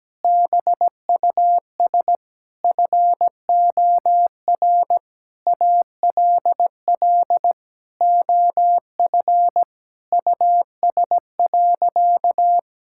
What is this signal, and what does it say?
Morse code, heard as a keyed tone, about 17 wpm, 700 Hz. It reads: BUS FOR ALL OF US.
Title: Sentences from Top 1000 Words - Review Part 6 - 17wpm